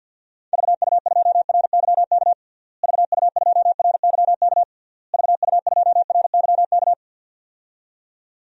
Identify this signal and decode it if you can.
Morse code, heard as a keyed tone, about 45 wpm, 700 Hz. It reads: VU2RYK VU2RYK VU2RYK
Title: DXCC-100 Call Signs - 6 Characters - Repeated 3x - 50wpm